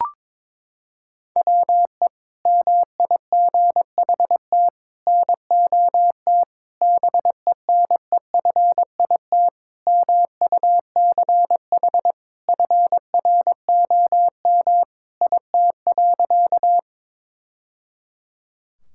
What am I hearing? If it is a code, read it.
Morse code, heard as a keyed tone, about 22 wpm, 700 Hz. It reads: WE MIGHT NOT BENEFIT MUCH FROM IT.
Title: Sentences from Top 1000 Words - Review Part 6 - 22wpm